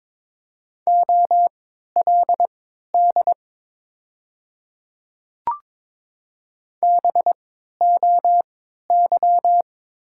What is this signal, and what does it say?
Morse code, heard as a keyed tone, about 22 wpm, spaced as for 13 wpm, 700 Hz. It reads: OLD BOY
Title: 3x Character Spacing - Top 200 Words - 22wpm